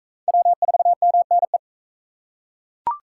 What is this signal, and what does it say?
Morse code, heard as a keyed tone, about 39 wpm, 700 Hz. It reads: W4MNE